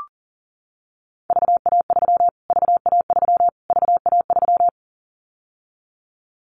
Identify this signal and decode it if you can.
Morse code, heard as a keyed tone, about 40 wpm, 700 Hz. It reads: VA3 VA3 VA3
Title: Callsign Prefixes for Top 25 DXCC - Repeated 3x - 40wpm